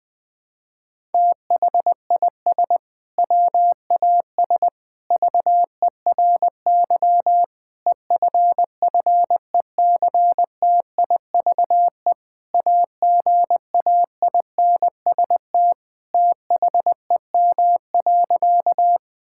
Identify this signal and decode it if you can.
Morse code, heard as a keyed tone, about 20 wpm, 700 Hz. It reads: THIS WAS VERY EFFECTIVE AGAINST THEM.
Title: Sentences from Top 1000 Words - Review Part 6 - 20wpm